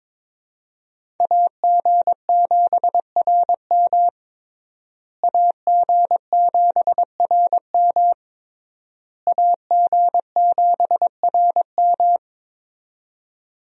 Morse code, heard as a keyed tone, about 22 wpm, 700 Hz. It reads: AG7RM AG7RM AG7RM
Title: USA Call Signs - 5 Characters - Repeated 3x - 22wpm